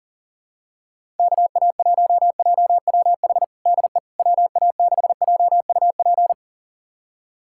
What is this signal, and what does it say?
Morse code, heard as a keyed tone, about 40 wpm, 700 Hz. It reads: KA1JWH DE WA6JUP
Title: USA Callsign DE Callsign - 6 Characters - 40wpm